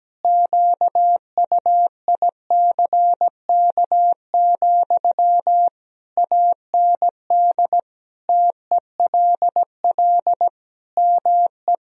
Morse code, heard as a keyed tone, about 17 wpm, 700 Hz. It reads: QUICK, AND TELL ME
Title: Sentences from Top 1000 Words - Review Part 3 - 17wpm